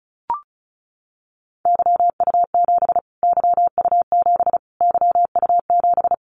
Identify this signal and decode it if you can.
Morse code, heard as a keyed tone, about 34 wpm, 700 Hz. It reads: YU7 YU7 YU7
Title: Callsign Prefixes for Top 25 DXCC - Repeated 3x - 35wpm